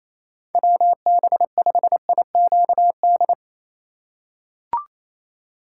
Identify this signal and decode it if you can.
Morse code, heard as a keyed tone, about 28 wpm, 700 Hz. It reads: WB5IQD
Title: USA Call Signs - 4x Character Spacing - 28wpm